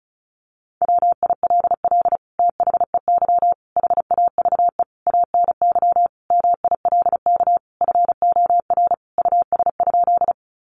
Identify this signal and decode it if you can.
Morse code, heard as a keyed tone, about 34 wpm, 700 Hz. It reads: WILL THEY HAVE ANY MILK FOR US?
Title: Sentences from Top 1500 Words 35wpm